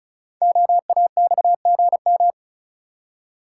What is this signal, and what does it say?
Morse code, heard as a keyed tone, about 36 wpm, 690 Hz. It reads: OAXGM
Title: Code Groups - 5 Letters - 35wpm